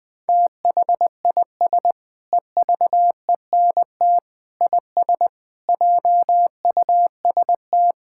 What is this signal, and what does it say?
Morse code, heard as a keyed tone, about 20 wpm, 700 Hz. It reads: THIS EVENT IS JUST
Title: Sentences with 8-Word Phrases - 20wpm